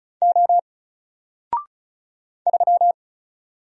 Morse code, heard as a keyed tone, about 36 wpm, 700 Hz. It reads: O 3
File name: Single-Letter-Number-35wpm